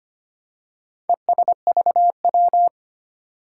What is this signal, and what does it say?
Morse code, heard as a keyed tone, about 25 wpm, 700 Hz. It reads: ESVW
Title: Four Challenging Letter-Number Combinations 25wpm